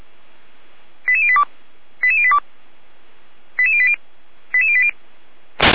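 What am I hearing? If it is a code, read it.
Five-tone selective call (ZVEI), heard as quick runs of five tones, tones of about 70 ms. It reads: 88082 88082 88080 88080
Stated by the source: ZVEI five-tone call received by SDR.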